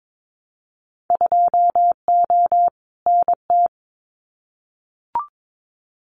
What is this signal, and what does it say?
Morse code, heard as a keyed tone, about 22 wpm, 700 Hz. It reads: 2O NT
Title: Winter Field Day Exchanges Encore 22wpm